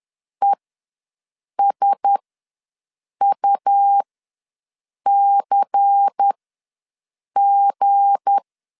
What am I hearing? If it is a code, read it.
Morse code, heard as a keyed tone, about 10 wpm, spaced as for 6 wpm, 800 Hz. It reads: ESUCG
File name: LEZIONI / 7 GF9Z / 7 G ws 10wpm 3843